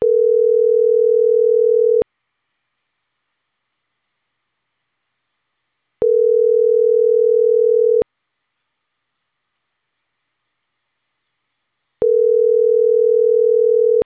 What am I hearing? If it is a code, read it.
Ringback tone (North America), 440 + 480 Hz.